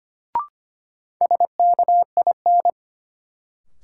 Morse code, heard as a keyed tone, about 25 wpm, 700 Hz. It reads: SKIN
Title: Sentences from Top 1300 Words - New Word Drill Alt - 25wpm